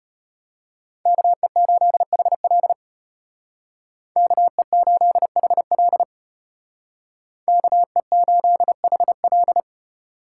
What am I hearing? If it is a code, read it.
Morse code, heard as a keyed tone, about 32 wpm, 700 Hz. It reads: KE8HL KE8HL KE8HL